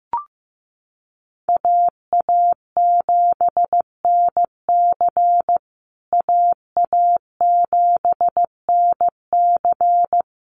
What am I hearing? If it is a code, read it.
Morse code, heard as a keyed tone, about 14 wpm, 700 Hz. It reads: AA7NC AA7NC
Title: USA Call Signs Repeated 3x 15wpm